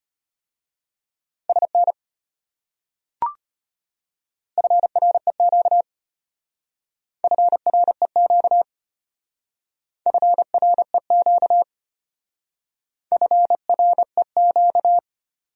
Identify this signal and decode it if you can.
Morse code, heard as a keyed tone, about 33 wpm, 700 Hz. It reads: SN FREQ FREQ FREQ FREQ